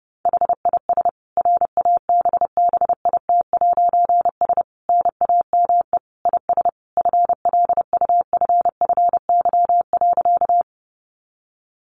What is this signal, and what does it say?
Morse code, heard as a keyed tone, about 30 wpm, 700 Hz. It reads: HIS RABBIT'S NAME IS FLUFFY.
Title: Easy Phrases 30wpm